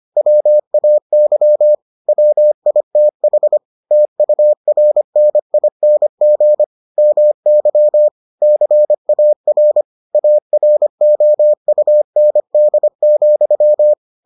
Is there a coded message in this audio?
Yes: Morse code from 0.2 to 13.9 s, heard as a keyed tone, about 25 wpm, 600 Hz. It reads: WAY WITH TURNING MY CAR AROUND,